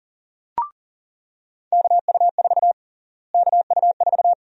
Morse code, heard as a keyed tone, about 37 wpm, 700 Hz. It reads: KU4 KU4
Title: USA Call Sign Prefixes - Repeated 2x - 40wpm